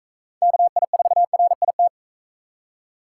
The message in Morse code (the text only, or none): KI4RIT